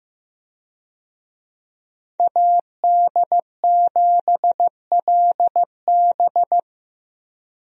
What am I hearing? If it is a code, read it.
Morse code, heard as a keyed tone, about 15 wpm, 700 Hz. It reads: AD7LB
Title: USA Call Signs - 5 Characters - Repeated 3x - 15wpm